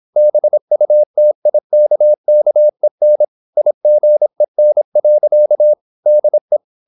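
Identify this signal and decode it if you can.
Morse code, heard as a keyed tone, about 26 wpm, 600 Hz. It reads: BUTIKKEN IGEN. DE
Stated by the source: High-speed (QRQ) Morse at 26 wpm.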